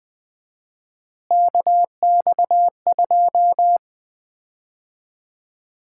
Morse code, heard as a keyed tone, about 21 wpm, 700 Hz. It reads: KX2
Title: Common QSO Phrases - 4x Word-Spacing - 20wpm